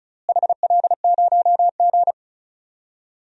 Morse code, heard as a keyed tone, about 36 wpm, 690 Hz. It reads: HL0G
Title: DXCC-100 Call Signs - 4 Characters - Repeated 3x - 35wpm